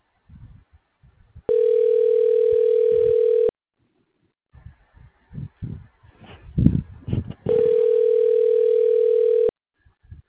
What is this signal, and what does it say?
Ringback tone (North America), 440 + 480 Hz.
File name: recordings / RE72b c122d e181caccb